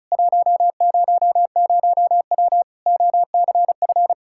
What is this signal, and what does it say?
Morse code, heard as a keyed tone, about 36 wpm, 700 Hz. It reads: 100W OCF
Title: K1USN SST QSO Practice 35wpm